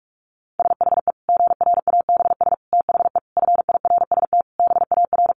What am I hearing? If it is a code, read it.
Morse code, heard as a keyed tone, about 44 wpm, 700 Hz. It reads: SHE GRABS THE FIRST BAR
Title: Easy Phrases 45wpm